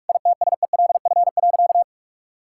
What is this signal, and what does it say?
Morse code, heard as a keyed tone, about 41 wpm, 700 Hz. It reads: ITSELF.
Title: Sentences from Top 1000 Words - Review Part 5 - 45wpm